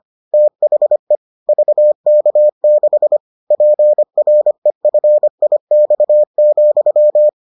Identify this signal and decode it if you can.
Morse code, heard as a keyed tone, about 25 wpm, 600 Hz. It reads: THE VK6 PREFIX,